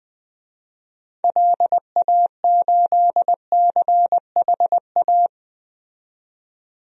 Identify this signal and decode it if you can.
Morse code, heard as a keyed tone, about 20 wpm, 700 Hz. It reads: LA8CHA
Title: DXCC-100 Call Signs - 6 Characters - 20wpm